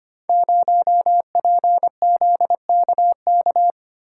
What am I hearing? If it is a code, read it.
Morse code, heard as a keyed tone, about 25 wpm, 700 Hz. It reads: 0PZKK
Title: Code Groups - 5 Letter:Numbers - 25wpm